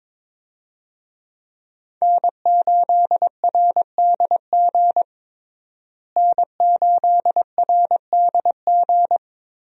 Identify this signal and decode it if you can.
Morse code, heard as a keyed tone, about 22 wpm, 700 Hz. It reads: N8RDG N8RDG
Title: USA Call Signs - 5 Characters - Repeated 2x - 22wpm